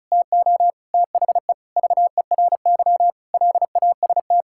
Morse code, heard as a keyed tone, about 34 wpm, 700 Hz. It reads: TO THE VERY LAST
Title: Sentences from Top 1600 Words 35wpm